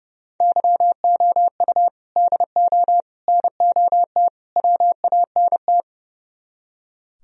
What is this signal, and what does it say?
Morse code, heard as a keyed tone, about 37 wpm, 700 Hz. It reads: YOU DO NOT WANT
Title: Speed-Racing - Sets of 4 Words 30wpm